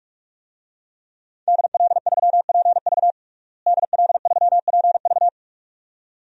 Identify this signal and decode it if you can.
Morse code, heard as a keyed tone, about 42 wpm, 700 Hz. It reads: DL3PV DL3PV
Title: DXCC-100 Call Signs - 5 Characters - Repeated 2x - 45wpm